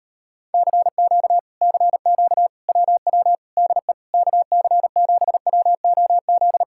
Morse code, heard as a keyed tone, about 38 wpm, 700 Hz. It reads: CQ CQ WW DE KC7WOZ